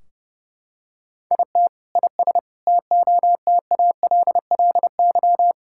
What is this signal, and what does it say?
Morse code, heard as a keyed tone, about 30 wpm, 700 Hz. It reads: IT IS TOTALLY
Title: Sentences from Top 1600 Words 30wpm